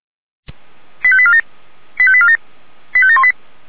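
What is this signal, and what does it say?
Five-tone selective call (ZVEI), heard as quick runs of five tones, tones of about 70 ms. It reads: 85847 85847 85718
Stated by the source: ZVEI five-tone selective call 85942, received by SDR.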